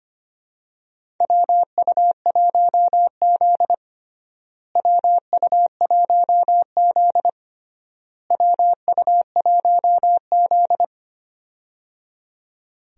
Morse code, heard as a keyed tone, about 25 wpm, 700 Hz. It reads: WU1Z WU1Z WU1Z
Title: USA Call Signs - 4 Characters - Repeated 3x - 25wpm